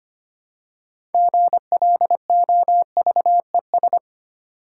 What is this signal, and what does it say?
Morse code, heard as a keyed tone, about 25 wpm, 690 Hz. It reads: GLOVES
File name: practice / 25wpm / Easy Words with Hints - 25wpm